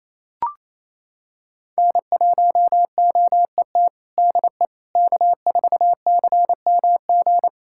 Morse code, heard as a keyed tone, about 27 wpm, 700 Hz. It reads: N1OET DE K4CMG
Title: USA Callsign DE Callsign - 5 Characters - 28wpm